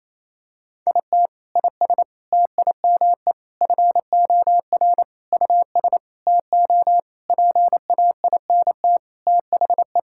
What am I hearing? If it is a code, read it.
Morse code, heard as a keyed tone, about 28 wpm, 700 Hz. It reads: IT IS TIME FOR US TO PAINT THE
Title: Sentences from Top 1000 Words - Review Part 6 - 28wpm